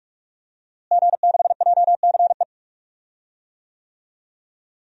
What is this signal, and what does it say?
Morse code, heard as a keyed tone, about 41 wpm, 700 Hz. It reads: G6JCE
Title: DXCC-100 Call Signs - 5 Characters - Repeated 2x - 45wpm